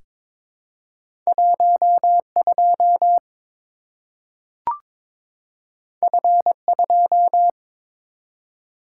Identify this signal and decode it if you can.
Morse code, heard as a keyed tone, about 23 wpm, 700 Hz. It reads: 12 F2
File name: practice / 22wpm / Two Letter-Number Combinations - 22wpm